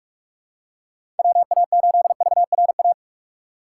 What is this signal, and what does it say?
Morse code, heard as a keyed tone, about 46 wpm, 700 Hz. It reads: WA8VRA
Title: USA Call Signs - 6 Characters - 3x Character Spacing - 45wpm